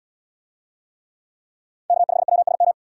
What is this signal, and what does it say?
Morse code, heard as a keyed tone, about 76 wpm, 700 Hz. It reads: K5WSM